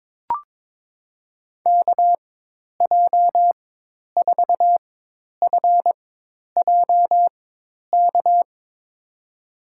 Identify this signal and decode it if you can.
Morse code, heard as a keyed tone, about 22 wpm, spaced as for 10 wpm, 700 Hz. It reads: KJ4FJK
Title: USA Call Signs - 4x Character Spacing - 22wpm